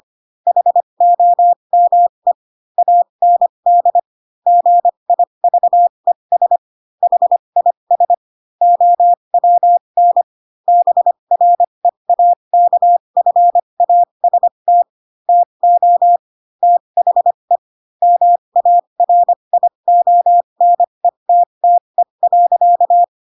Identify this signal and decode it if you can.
Morse code, heard as a keyed tone, about 25 wpm, spaced as for 22 wpm, 700 Hz. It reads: HOME AND GIVES HIS OWN BREAKFAST TO THE MARIONETTE.
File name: Pinocchio / Pinocchio ch7 18 25 ik1hge